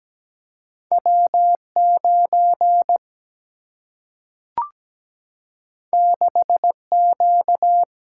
Morse code, heard as a keyed tone, about 17 wpm, 700 Hz. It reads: W 9 6 Q
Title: Single-Letter Single-Number 17wpm